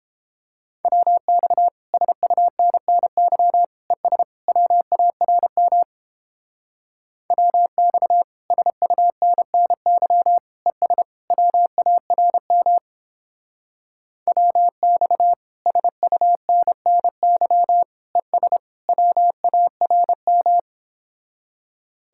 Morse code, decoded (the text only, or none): WX SUNNY ES WARM WX SUNNY ES WARM WX SUNNY ES WARM